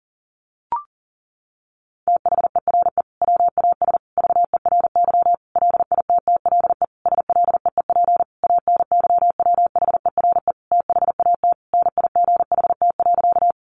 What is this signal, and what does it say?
Morse code, heard as a keyed tone, about 37 wpm, 700 Hz. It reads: THERE WAS VERY LITTLE SLEEP ANYWHERE THAT NIGHT.
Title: Sentences from Top 1400 Words 40wpm